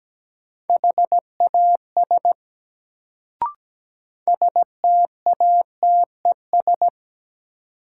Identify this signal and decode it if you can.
Morse code, heard as a keyed tone, about 17 wpm, 700 Hz. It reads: HAS STATES
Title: Sets of 2 Words - New Word Drill - 17wpm